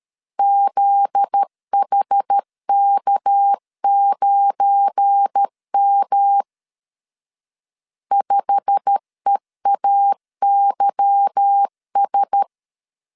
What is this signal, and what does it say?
Morse code, heard as a keyed tone, about 13 wpm, 800 Hz. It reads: ZHK9M 5EAYS
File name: CW / DET / 1 9 1500C S A 12wpm 3800